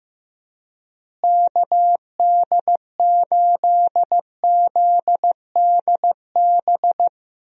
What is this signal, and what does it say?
Morse code, heard as a keyed tone, about 15 wpm, 700 Hz. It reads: KD8ZDB
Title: USA Call Signs - 6 Characters - Repeated 3x - 15wpm